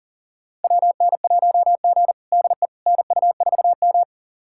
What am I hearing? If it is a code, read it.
Morse code, heard as a keyed tone, about 40 wpm, 700 Hz. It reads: WN1G DE NU4M
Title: USA Callsign DE Callsign - 4 Characters - 40wpm